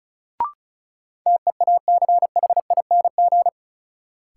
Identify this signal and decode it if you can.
Morse code, heard as a keyed tone, about 36 wpm, 700 Hz. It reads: TEACHING
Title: Sentences from Top 1100 Words - New Word Drill - 35wpm